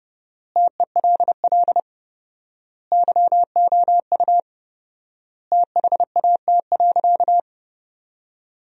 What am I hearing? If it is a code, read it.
Morse code, heard as a keyed tone, about 30 wpm, 700 Hz. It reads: TELL YOU THAT.
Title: 4x Word-Spacing - Sentences from Top 300 Words - 30wpm